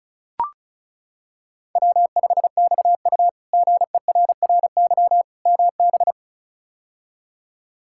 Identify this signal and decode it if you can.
Morse code, heard as a keyed tone, about 36 wpm, 700 Hz. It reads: W5XU GERRY MB
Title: K1USN SST QSO Practice - 35wpm